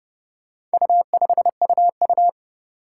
Morse code, heard as a keyed tone, about 30 wpm, 700 Hz. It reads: U5UU
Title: Four Challenging Letter-Number Combinations 30wpm